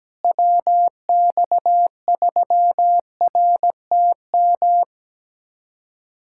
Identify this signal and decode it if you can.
Morse code, heard as a keyed tone, about 17 wpm, 700 Hz. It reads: WX3RTM
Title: USA Callsign DE Callsign - 6 Characters - 17wpm